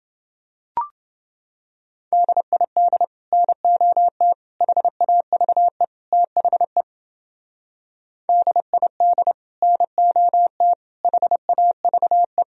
Morse code, heard as a keyed tone, about 30 wpm, 700 Hz. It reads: DID NOT HAVE THE DID NOT HAVE